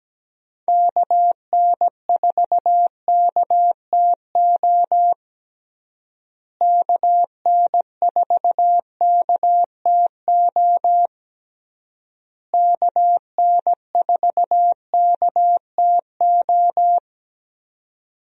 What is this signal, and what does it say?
Morse code, heard as a keyed tone, about 17 wpm, 700 Hz. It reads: KN4KTO KN4KTO KN4KTO
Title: USA Call Signs - 6 Characters - Repeated 3x - 17wpm